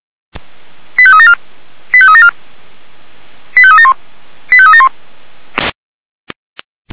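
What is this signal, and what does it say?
Five-tone selective call (ZVEI), heard as quick runs of five tones, tones of about 70 ms. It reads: 85374 85374 85371 85371
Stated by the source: ZVEI five-tone call 89468 received by SDR.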